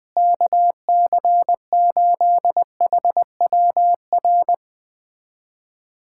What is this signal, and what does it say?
Morse code, heard as a keyed tone, about 20 wpm, 700 Hz. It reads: KC8HWR
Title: USA Callsign DE Callsign 20wpm